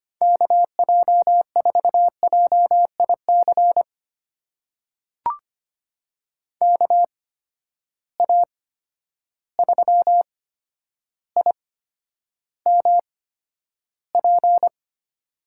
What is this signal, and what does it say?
Morse code, heard as a keyed tone, about 25 wpm, spaced as for 11 wpm, 700 Hz. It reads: KJ4JIC KA3IMP
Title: USA Call Signs - 6 Characters - 8x Character Spacing - 25wpm